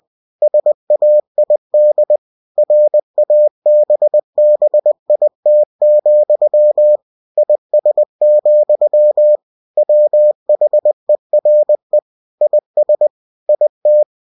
Morse code, heard as a keyed tone, about 20 wpm, 600 Hz. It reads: SAID RABBIT, IS, WHERE IS IT
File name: cw-books / winnie-the-pooh / winnie-20wpm-page-0085